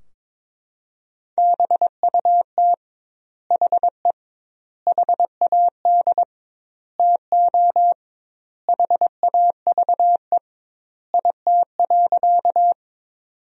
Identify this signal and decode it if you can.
Morse code, heard as a keyed tone, about 22 wpm, 700 Hz. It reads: BUT HE HAD TO HAVE IT.
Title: 2x Word-Spacing - Sentences from Top 100 Words - 22wpm